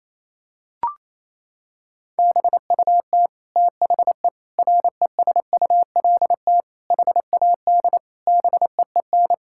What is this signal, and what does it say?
Morse code, heard as a keyed tone, about 28 wpm, 700 Hz. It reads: BUT THE RESULT HAD BEEN
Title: Sentences from Top 1000 Words - Review Part 5 - 28wpm